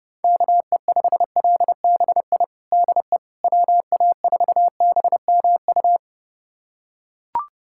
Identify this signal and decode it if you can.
Morse code, heard as a keyed tone, about 32 wpm, 700 Hz. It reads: KE5LBI DE WA4BMU
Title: USA Callsign DE Callsign - 6 Characters - 30wpm